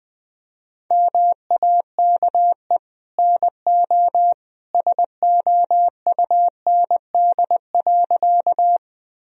Morse code, heard as a keyed tone, about 19 wpm, 700 Hz. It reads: MAKE NO SOUND.
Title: Sentences from Top 1000 Words - Review Part 1 - 20wpm